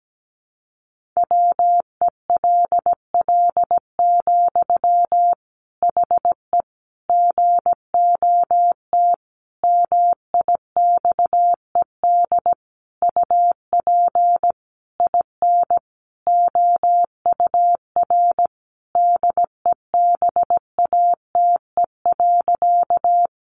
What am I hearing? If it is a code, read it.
Morse code, heard as a keyed tone, about 17 wpm, 700 Hz. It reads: WELL, HE GOT MIXED UP IN OUR DEBATE.
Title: Sentences from Top 1500 Words 17wpm